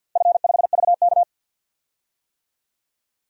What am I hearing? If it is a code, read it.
Morse code, heard as a keyed tone, about 53 wpm, 700 Hz. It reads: F5VK
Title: DXCC-100 Call Signs - 4 Characters - 50wpm